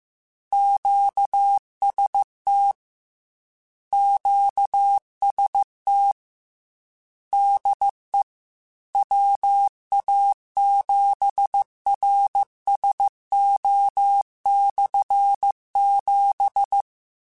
Morse code, heard as a keyed tone, about 15 wpm, 800 Hz. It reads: QST QST DE WA7RSO/7
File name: Combinations / Combinations-mp3s / Combinations@13